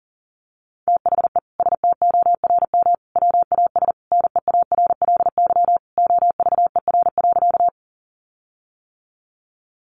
Morse code, heard as a keyed tone, about 40 wpm, 700 Hz. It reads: THE STORM WAS NEARLY OVER.
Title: Sentences from Top 1500 Words 40wpm